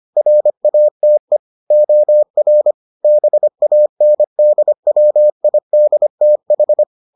Morse code, heard as a keyed tone, about 25 wpm, 600 Hz. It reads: RATE OR BANDWIDTH